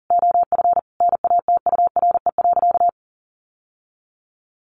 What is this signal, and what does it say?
Morse code, heard as a keyed tone, about 40 wpm, 700 Hz. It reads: OF NATURE.
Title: Sentences from Top 1200 Words 40wpm